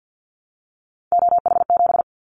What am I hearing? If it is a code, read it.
Morse code, heard as a keyed tone, about 50 wpm, 700 Hz. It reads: OH7
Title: Callsign Prefixes for Top 25 DXCC - 50wpm